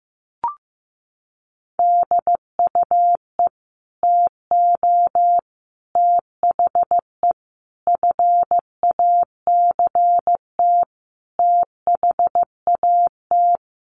Morse code, heard as a keyed tone, about 15 wpm, 700 Hz. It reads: DUE TO THE FACT THAT
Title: Sets of 5 Words Encore 15wpm